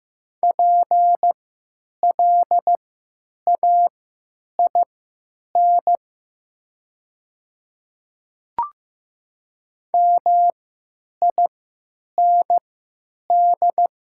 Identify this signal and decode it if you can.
Morse code, heard as a keyed tone, about 15 wpm, spaced as for 9 wpm, 700 Hz. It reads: PLAIN MIND
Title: 3x Character Spacing - Top 500 Words - 15wpm